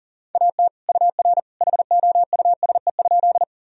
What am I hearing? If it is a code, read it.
Morse code, heard as a keyed tone, about 40 wpm, 700 Hz. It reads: AT UR HOUSE?
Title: Sentences from Top 200 Words and Abbreviations 40wpm